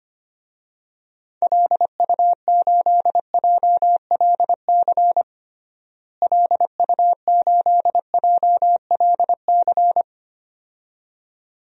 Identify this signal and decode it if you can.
Morse code, heard as a keyed tone, about 25 wpm, 700 Hz. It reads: LU8JLC LU8JLC
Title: DXCC-100 Call Signs - 6 Characters - Repeated 2x - 25wpm